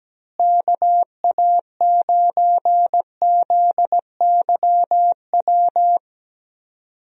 Morse code, heard as a keyed tone, about 17 wpm, 700 Hz. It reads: KA9ZYW
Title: USA Call Signs - 6 Characters - Repeated 3x - 17wpm